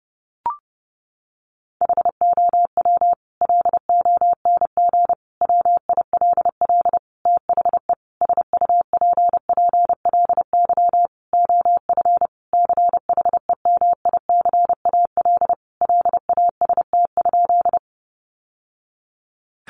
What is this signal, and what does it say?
Morse code, heard as a keyed tone, about 30 wpm, 700 Hz. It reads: HOW LONG WILL THE SUPPLY OF CHEMICAL LAST?
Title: Sentences from Top 1200 Words 30wpm